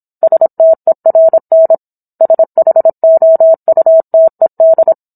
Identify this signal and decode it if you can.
Morse code, heard as a keyed tone, about 27 wpm, 650 Hz. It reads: STERN SHOUTED